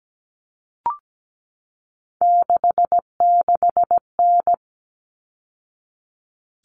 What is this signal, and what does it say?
Morse code, heard as a keyed tone, about 17 wpm, 700 Hz. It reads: 66N
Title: All Cut Numbers - 3-Digit 17wpm